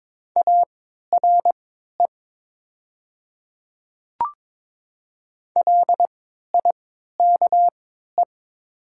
Morse code, heard as a keyed tone, about 21 wpm, spaced as for 12 wpm, 700 Hz. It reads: ARE LIKE
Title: 3x Character Spacing - Top 100 Words - 22wpm